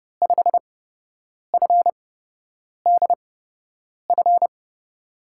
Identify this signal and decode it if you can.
Morse code, heard as a keyed tone, about 31 wpm, spaced as for 8 wpm, 700 Hz. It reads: 5FDF
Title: USA Call Signs - 6 Characters - 8x Character Spacing - 30wpm